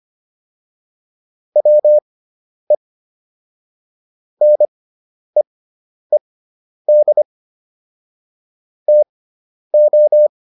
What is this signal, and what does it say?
Morse code, heard as a keyed tone, about 25 wpm, spaced as for 10 wpm, 600 Hz. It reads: WE NEED TO